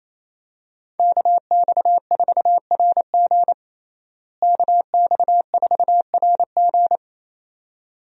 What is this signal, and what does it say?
Morse code, heard as a keyed tone, about 28 wpm, 700 Hz. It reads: KX4RG KX4RG
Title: USA Call Signs - 5 Characters - Repeated 2x - 28wpm